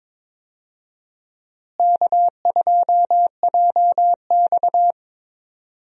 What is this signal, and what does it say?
Morse code, heard as a keyed tone, about 22 wpm, 700 Hz. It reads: K2JX
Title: USA Call Signs - 4 Characters - Repeated 2x - 22wpm